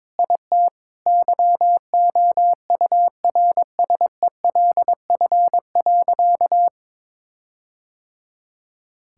Morse code, heard as a keyed tone, about 22 wpm, 700 Hz. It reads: IT YOURSELF.
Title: Sentences from Top 1600 Words 22wpm